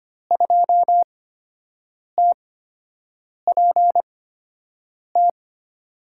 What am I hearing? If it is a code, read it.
Morse code, heard as a keyed tone, about 25 wpm, spaced as for 7 wpm, 700 Hz. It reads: 2TPT
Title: USA Call Signs - 5 Characters - 8x Character Spacing - 25wpm